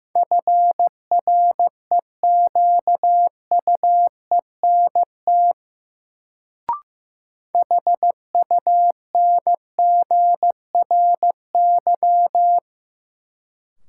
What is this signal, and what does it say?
Morse code, heard as a keyed tone, about 16 wpm, 700 Hz. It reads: FREQUENT HUNGRY
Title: Sentences from Top 1400 Words - New Word Drill - 15wpm